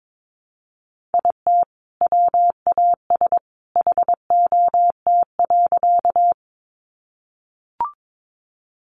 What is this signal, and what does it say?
Morse code, heard as a keyed tone, about 22 wpm, 700 Hz. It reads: IT WAS HOT.
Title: Sentences from Top 500 Words 22wpm